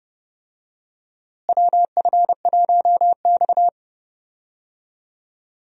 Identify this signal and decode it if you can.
Morse code, heard as a keyed tone, about 30 wpm, 700 Hz. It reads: WF1X
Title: USA Call Signs - 4 Characters - Repeated 2x - 30wpm